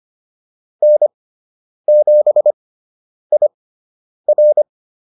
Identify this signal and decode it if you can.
Morse code, heard as a keyed tone, about 25 wpm, spaced as for 9 wpm, 600 Hz. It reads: N7IR
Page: fundamental-practice-files